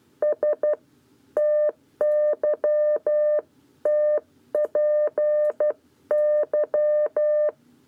Morse code, heard as a keyed tone, about 12 wpm, 600 Hz. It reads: S TYTPY
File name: 7b lezione cw 1